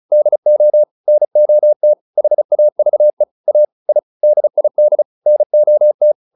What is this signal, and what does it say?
Morse code, heard as a keyed tone, about 34 wpm, 600 Hz. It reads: DO NOT HAVE A I DID NOT